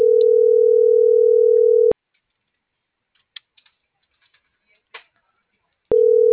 Ringback tone (North America), 440 + 480 Hz.